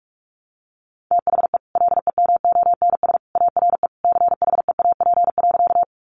Morse code, heard as a keyed tone, about 44 wpm, 700 Hz. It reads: THE LEMONS ARE CHEAP.
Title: Additional Easy Phrases 45wpm